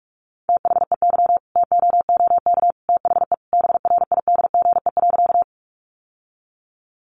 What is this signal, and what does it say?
Morse code, heard as a keyed tone, about 44 wpm, 700 Hz. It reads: THEY TOOK THE BRIDGE.
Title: Sentences from Top 1200 Words 45wpm